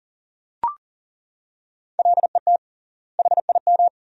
Morse code, heard as a keyed tone, about 40 wpm, 700 Hz. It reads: LET HIM
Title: 3x Word-Spacing - Sentences from Top 300 Words - 40wpm